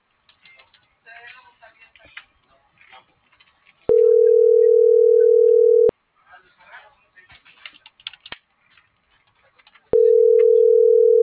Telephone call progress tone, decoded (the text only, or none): ringback tone